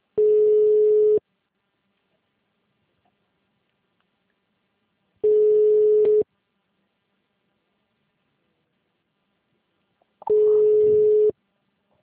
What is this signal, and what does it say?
Ringback tone (Europe, 425 Hz).